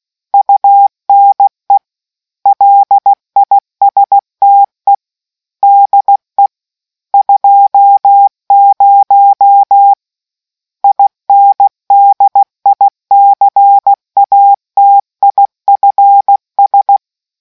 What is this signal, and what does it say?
Morse code, heard as a keyed tone, about 16 wpm, 800 Hz. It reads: UNE LISTE DE 20 INDICATIFS